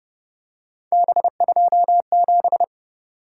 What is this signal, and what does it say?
Morse code, heard as a keyed tone, about 31 wpm, 700 Hz. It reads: B27
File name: Speed-Racing - Three Letter-Number Combinations - 25wpm